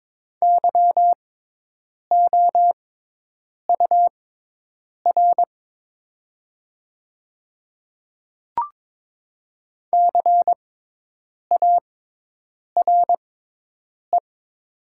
Morse code, heard as a keyed tone, about 23 wpm, spaced as for 8 wpm, 700 Hz. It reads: YOUR CARE
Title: 6x Character Spacing - Top 300 Words - 22wpm